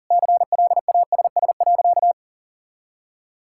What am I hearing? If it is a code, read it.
Morse code, heard as a keyed tone, about 40 wpm, 700 Hz. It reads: CLASS.
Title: Sentences from Top 1000 Words - Review Part 6 - 40wpm